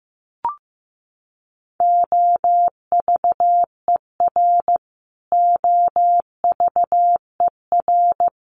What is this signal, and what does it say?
Morse code, heard as a keyed tone, about 15 wpm, 700 Hz. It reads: OVER OVER
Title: Top 200 Words Repeated 15wpm